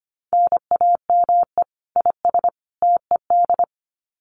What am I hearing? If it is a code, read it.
Morse code, heard as a keyed tone, about 25 wpm, 700 Hz. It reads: NAME IS TED
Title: Common QSO Phrases 25wpm